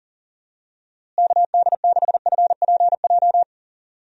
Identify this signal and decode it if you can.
Morse code, heard as a keyed tone, about 40 wpm, 700 Hz. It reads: KD6FPJ
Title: USA Call Signs - 6 Characters - Repeated 3x - 40wpm